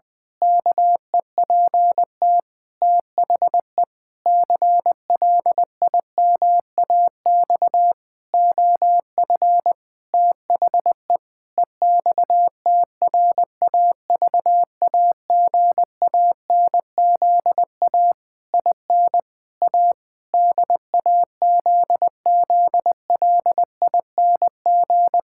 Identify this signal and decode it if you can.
Morse code, heard as a keyed tone, about 20 wpm, 700 Hz. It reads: KEPT THE CLIMAX OF THE EXTRAVAGANZA IN A DAZZLING
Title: Pangram Challenge v2 - 20wpm